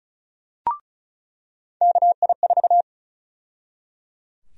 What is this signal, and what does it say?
Morse code, heard as a keyed tone, about 34 wpm, 700 Hz. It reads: KI4
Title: USA Call Sign Prefixes - 35wpm